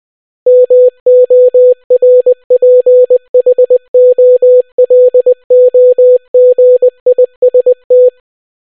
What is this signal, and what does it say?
Morse code, heard as a keyed tone, about 20 wpm, 500 Hz. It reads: MORPHOLOGIST